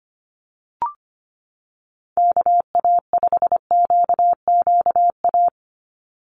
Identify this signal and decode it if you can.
Morse code, heard as a keyed tone, about 25 wpm, 700 Hz. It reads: KA5QQA
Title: NA SOTA Chaser Callsigns 25wpm